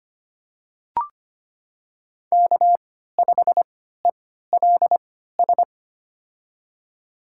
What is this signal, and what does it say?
Morse code, heard as a keyed tone, about 25 wpm, spaced as for 14 wpm, 700 Hz. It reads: K5ELS
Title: USA Call Signs - 5 Characters - 3x Character Spacing - 25wpm